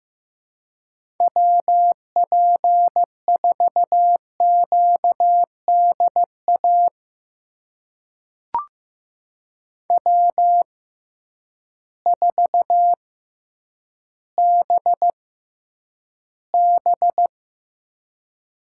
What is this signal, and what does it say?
Morse code, heard as a keyed tone, about 15 wpm, spaced as for 9 wpm, 700 Hz. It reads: WP4QDA W4BB
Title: USA Call Signs - 6x Character Spacing - 15wpm